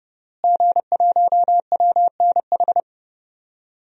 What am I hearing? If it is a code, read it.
Morse code, heard as a keyed tone, about 30 wpm, 700 Hz. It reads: G1WNH
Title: DXCC-100 Call Signs - 5 Characters - Repeated 3x - 30wpm